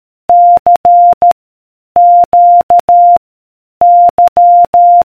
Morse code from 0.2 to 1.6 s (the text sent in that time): C